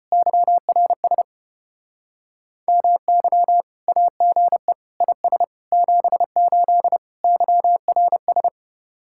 Morse code, heard as a keyed tone, about 31 wpm, 700 Hz. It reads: YRS MY AGE IS 78 YRS
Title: Speed-Racing Common QSO Phrases - 25wpm